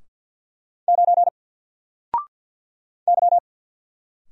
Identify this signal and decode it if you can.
Morse code, heard as a keyed tone, about 50 wpm, 700 Hz. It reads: 9 Y
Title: Single Letter-Number Alt 50wpm